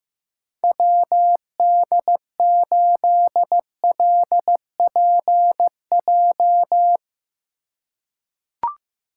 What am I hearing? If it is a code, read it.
Morse code, heard as a keyed tone, about 15 wpm, 700 Hz. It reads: WD8LPJ
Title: USA Call Signs - 6 Characters - 3x Character Spacing - 15wpm